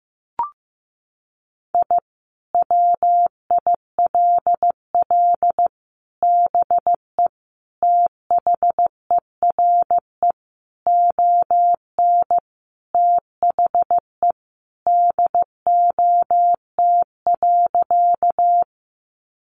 Morse code, heard as a keyed tone, about 15 wpm, 700 Hz. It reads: I WILL BE THERE ON THE DOT.
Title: Sentences from Top 1500 Words 15wpm